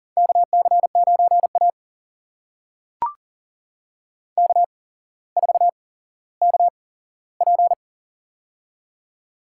Morse code, heard as a keyed tone, about 38 wpm, spaced as for 17 wpm, 700 Hz. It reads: KC9A K4KP
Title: USA Call Signs - 4 Characters - 8x Character Spacing - 40wpm